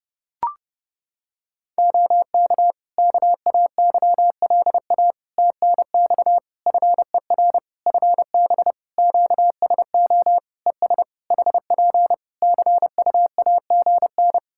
Morse code, heard as a keyed tone, about 30 wpm, 700 Hz. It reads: OK KAYLA TNX FER FB QSO ES HP CUAGN
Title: K1USN SST QSO Practice 30wpm